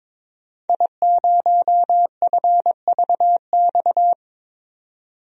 Morse code, heard as a keyed tone, about 22 wpm, 700 Hz. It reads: I0FVX
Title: DXCC-100 Call Signs - 5 Characters - 22wpm